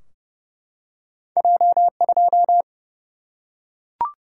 Morse code, heard as a keyed tone, about 31 wpm, 700 Hz. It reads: J2
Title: Single-Letter Single-Number 30wpm